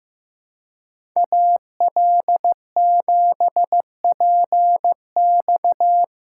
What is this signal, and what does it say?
Morse code, heard as a keyed tone, about 15 wpm, 700 Hz. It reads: AL7PX
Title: USA Call Signs - 5 Characters - 3x Character Spacing - 15wpm